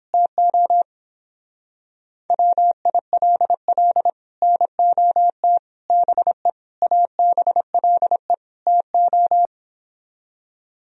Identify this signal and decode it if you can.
Morse code, heard as a keyed tone, about 29 wpm, 700 Hz. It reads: TO WILL NOT BE ABLE TO